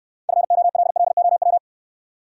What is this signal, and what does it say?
Morse code, heard as a keyed tone, about 68 wpm, 700 Hz. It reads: FOLLOW